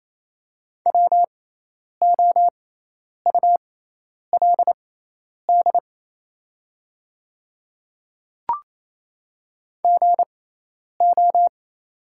Morse code, heard as a keyed tone, about 28 wpm, spaced as for 10 wpm, 700 Hz. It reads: WOULD GO
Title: 6x Character Spacing - Top 100 Words - 28wpm